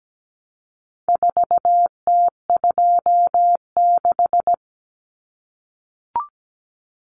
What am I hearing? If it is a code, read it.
Morse code, heard as a keyed tone, about 17 wpm, 700 Hz. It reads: 4T26
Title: Common Cut Numbers - 4-Digit 17wpm